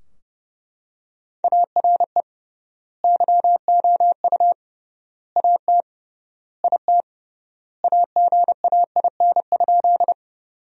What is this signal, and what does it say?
Morse code, heard as a keyed tone, about 30 wpm, 700 Hz. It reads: ARE YOU AT IT AGAIN?
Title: 3x Word-Spacing - Sentences from Top 200 Words - 30wpm